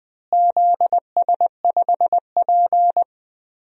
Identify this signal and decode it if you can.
Morse code, heard as a keyed tone, about 20 wpm, 700 Hz. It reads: ZS5P
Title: DXCC-100 Call Signs - 4 Characters - Repeated 2x - 20wpm